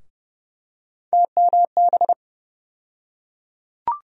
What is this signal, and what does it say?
Morse code, heard as a keyed tone, about 30 wpm, 700 Hz. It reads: TMB
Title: USA Call Sign Postfixes - 3 Letters 30wpm